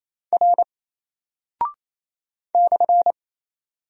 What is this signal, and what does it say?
Morse code, heard as a keyed tone, about 27 wpm, 700 Hz. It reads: R /
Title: Single Character (Letter:Number:Punctuation:Prosign) Alt 28wpm